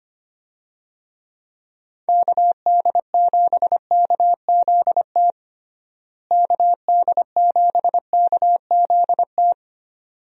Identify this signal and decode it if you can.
Morse code, heard as a keyed tone, about 25 wpm, 700 Hz. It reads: KD7KZT KD7KZT
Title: USA Call Signs - 6 Characters - Repeated 2x - 25wpm